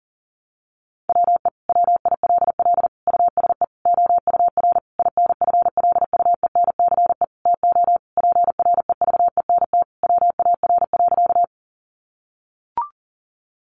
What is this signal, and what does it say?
Morse code, heard as a keyed tone, about 40 wpm, 700 Hz. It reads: WE WILL USE OUR INFLUENCE TO PREVENT WAR.
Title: Sentences from Top 1100 Words 40wpm